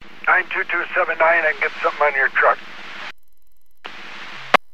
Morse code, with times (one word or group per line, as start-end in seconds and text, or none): none